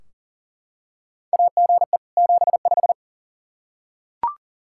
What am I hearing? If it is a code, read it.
Morse code, heard as a keyed tone, about 40 wpm, 700 Hz. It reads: AGE 75
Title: Practice QSO - Exchange 3 - More Info - 40wpm